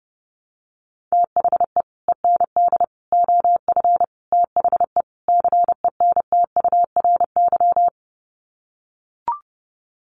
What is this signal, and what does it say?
Morse code, heard as a keyed tone, about 30 wpm, 700 Hz. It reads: THE END OF THE CENTURY
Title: Sets of 5 Words Encore2 30wpm